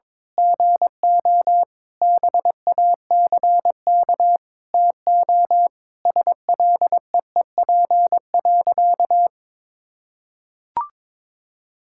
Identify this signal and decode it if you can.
Morse code, heard as a keyed tone, about 22 wpm, 700 Hz. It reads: GO BACK TO SLEEP.
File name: Sentences with 7-Word Phrases - 22wpm